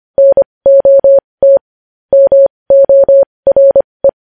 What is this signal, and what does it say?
Morse code, heard as a keyed tone, about 25 wpm, spaced as for 20 wpm, 550 Hz. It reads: NOT MORE
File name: podcast / News headlines-1x-25x20